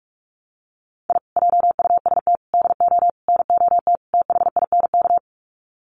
Morse code, heard as a keyed tone, about 44 wpm, 700 Hz. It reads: I JUST DO NOT THINK
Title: Sets of 5 Words Encore 45wpm